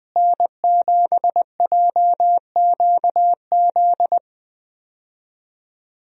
Morse code, heard as a keyed tone, about 20 wpm, 700 Hz. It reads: N7JQZ
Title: USA Call Signs - 5 Characters - Repeated 3x - 20wpm